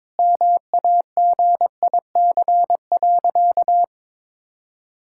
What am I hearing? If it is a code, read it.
Morse code, heard as a keyed tone, about 22 wpm, 700 Hz. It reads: MAGIC.
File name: practice / 22wpm / Sentences with 3-Word Phrases - 22wpm